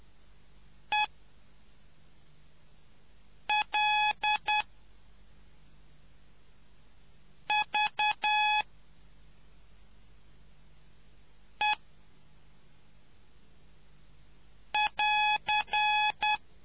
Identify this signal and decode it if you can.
Morse code, heard as a keyed tone, about 10 wpm, 2550 Hz. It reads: E L V E <AR>